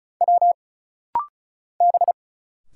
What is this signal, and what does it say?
Morse code, heard as a keyed tone, about 36 wpm, 700 Hz. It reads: W B
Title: Single Letter-Number Alt - Rapid-Fire 35wpm